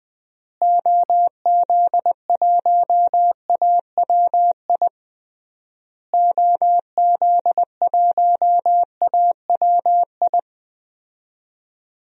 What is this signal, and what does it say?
Morse code, heard as a keyed tone, about 21 wpm, 700 Hz. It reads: OZ1AWI OZ1AWI
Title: DXCC-100 Call Signs - 6 Characters - Repeated 3x - 20wpm